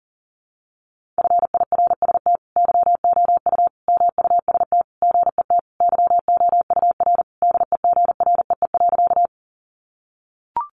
Morse code, heard as a keyed tone, about 40 wpm, 700 Hz. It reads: FIRST YOU MUST GET YOUR DEGREE.
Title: Sentences from Top 1500 Words 40wpm